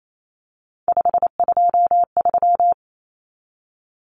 Morse code, heard as a keyed tone, about 28 wpm, 700 Hz. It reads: 523
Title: Numbers - 3-Digit 28wpm